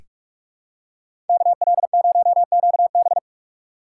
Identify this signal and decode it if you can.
Morse code, heard as a keyed tone, about 41 wpm, 700 Hz. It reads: KL0QB
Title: USA Call Signs - 5 Characters - 3x Character Spacing - 45wpm